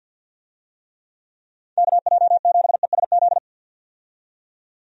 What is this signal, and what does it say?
Morse code, heard as a keyed tone, about 49 wpm, 700 Hz. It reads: KJ7ESZ
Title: USA Call Signs - 6 Characters - 50wpm